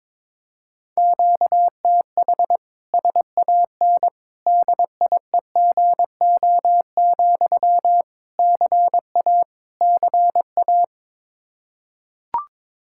Morse code, heard as a keyed tone, about 21 wpm, 700 Hz. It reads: QTH SAN DIEGO, CA CA
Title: Practice QSO - Exchange 1 - Beginning - 22wpm